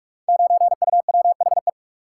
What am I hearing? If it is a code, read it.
Morse code, heard as a keyed tone, about 44 wpm, 700 Hz. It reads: 9UWHE
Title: Code Groups - 5 Letter:Numbers - 45wpm